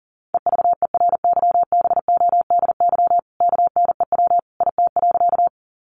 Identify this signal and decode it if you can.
Morse code, heard as a keyed tone, about 40 wpm, 700 Hz. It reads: EVERYBODY KNEW IT.
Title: Sentences from Top 1400 Words 40wpm